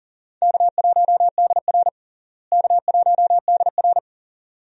Morse code, heard as a keyed tone, about 40 wpm, 700 Hz. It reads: K1DR K1DR
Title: USA Call Signs - 4 Characters - Repeated 2x - 40wpm